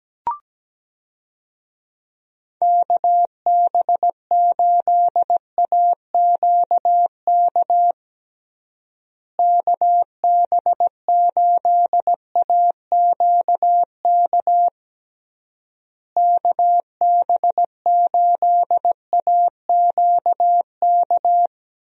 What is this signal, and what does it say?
Morse code, heard as a keyed tone, about 17 wpm, 700 Hz. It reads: KB8AQK KB8AQK KB8AQK
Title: USA Call Signs - 6 Characters - Repeated 3x - 17wpm